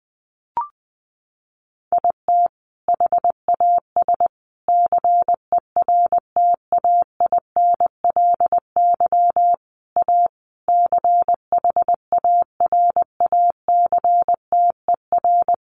Morse code, heard as a keyed tone, about 20 wpm, 700 Hz. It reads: IT HAS CERTAINLY A CHARACTER
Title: Sentences from Top 1200 Words 20wpm